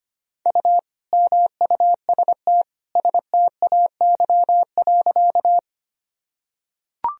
Morse code, heard as a keyed tone, about 25 wpm, 700 Hz. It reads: U MUST STAY.
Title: Sentences from Top 500 Words and Abbreviations 25wpm